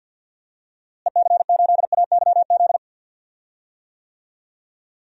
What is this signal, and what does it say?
Morse code, heard as a keyed tone, about 47 wpm, 700 Hz. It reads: EC8AYZ
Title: DXCC-100 Call Signs - 6 Characters - Repeated 3x - 50wpm